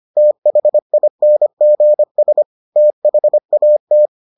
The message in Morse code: THINGS THAT

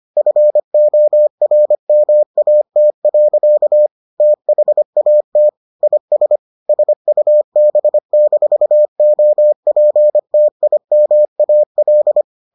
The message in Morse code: FORMAT. THAT IS SUB-OPTIMAL

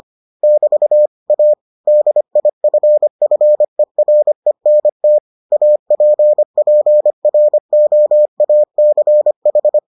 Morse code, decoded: <BT> A DIFFERENT APPROACH

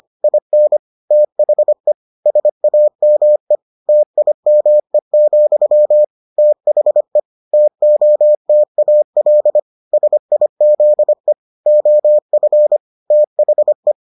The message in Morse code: IN THE SAME TIME, THE TOTAL SIZE OF THE

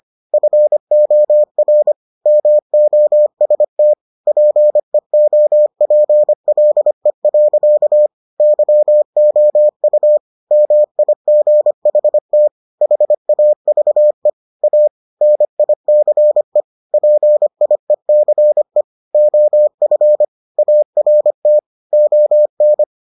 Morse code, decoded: FOR MOST PEOPLE. YOU MIGHT HAVE A NICE PIECE OF ART ON

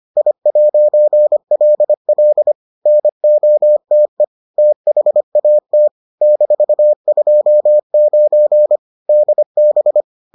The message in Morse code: I'LL NOTE THAT -29 DB

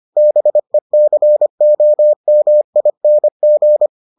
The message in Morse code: BECOMING

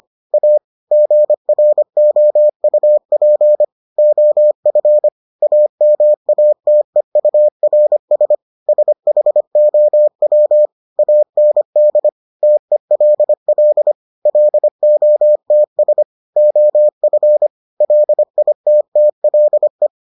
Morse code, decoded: A GROUP OF AMATEURS SHOW AND TELL LOTS OF LITTLE

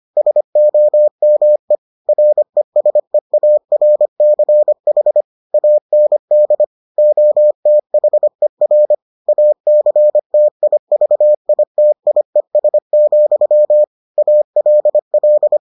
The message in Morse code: SOME RESEARCH AND OTHER ACTIVITIES, ALL

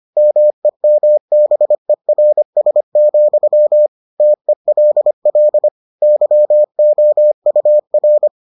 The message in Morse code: MEMBERS, TELL YOUR

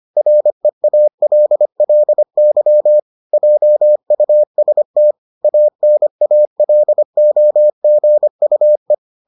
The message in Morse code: REALLY JUST ANALOGUE

